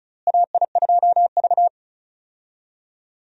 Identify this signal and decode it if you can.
Morse code, heard as a keyed tone, about 34 wpm, 700 Hz. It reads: AI2V
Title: USA Call Signs - 4 Characters - Repeated 3x - 35wpm